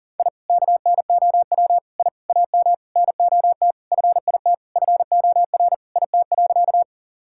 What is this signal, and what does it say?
Morse code, heard as a keyed tone, about 37 wpm, 700 Hz. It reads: I KNOW I AM NOT FIT FOR IT.
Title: Sentences from Top 1000 Words - Review Part 5 - 40wpm